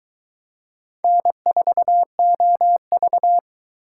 Morse code, heard as a keyed tone, about 22 wpm, 700 Hz. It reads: N4OV